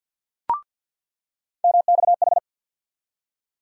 Morse code, heard as a keyed tone, about 48 wpm, 700 Hz. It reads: MXH
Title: USA Call Sign Postfixes - 3 Letters 50wpm